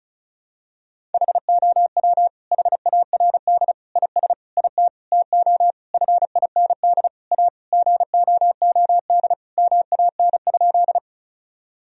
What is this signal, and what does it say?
Morse code, heard as a keyed tone, about 34 wpm, 700 Hz. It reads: HOW HARD IS IT TO FIND A GOOD MAN?
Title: Sentences from Top 1000 Words - Review Part 1 - 35wpm